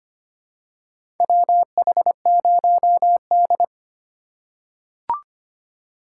Morse code, heard as a keyed tone, about 25 wpm, 700 Hz. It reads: WH0D
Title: USA Call Signs - 4 Characters - 2x Character Spacing - 25wpm